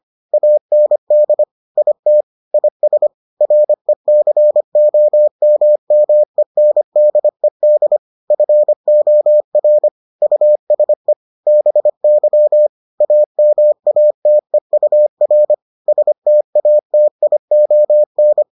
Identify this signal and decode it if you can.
Morse code, heard as a keyed tone, about 25 wpm, 600 Hz. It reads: AND IT IS RECOMMENDED FOR USE BY AMATEUR STATION